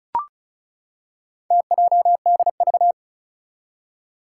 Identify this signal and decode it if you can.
Morse code, heard as a keyed tone, about 36 wpm, 700 Hz. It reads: TJDV
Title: Four Challenging Letter-Number Combinations 35wpm